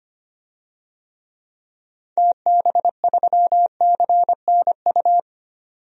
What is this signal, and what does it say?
Morse code, heard as a keyed tone, about 25 wpm, 700 Hz. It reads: TB3CNU
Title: DXCC-100 Call Signs - 6 Characters - 25wpm